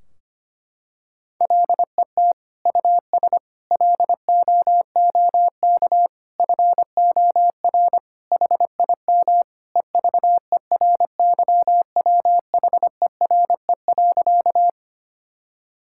Morse code, heard as a keyed tone, about 25 wpm, 700 Hz. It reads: LET US LOOK FOR HIM EVERYWHERE.
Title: Sentences from Top 1600 Words 25wpm